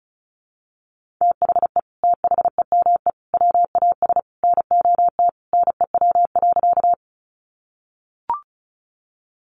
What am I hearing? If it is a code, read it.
Morse code, heard as a keyed tone, about 34 wpm, 700 Hz. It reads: THE THEME WAS NOT NEW.
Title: Sentences from Top 1100 Words 35wpm